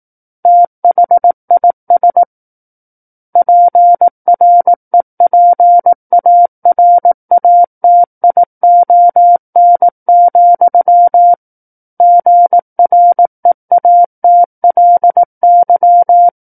Morse code, heard as a keyed tone, about 18 wpm, 700 Hz. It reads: THIS PREPARATION, GREATLY
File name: 18 WPM